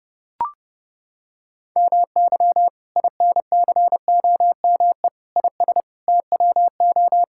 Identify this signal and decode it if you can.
Morse code, heard as a keyed tone, about 30 wpm, 700 Hz. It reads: MY INCOME IS TWO
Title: Sentences from Top 1000 Words - Review Part 5 - 30wpm